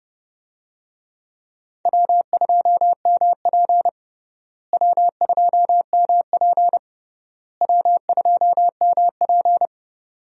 Morse code, heard as a keyed tone, about 30 wpm, 700 Hz. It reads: W2MP W2MP W2MP
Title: USA Call Signs - 4 Characters - Repeated 3x - 30wpm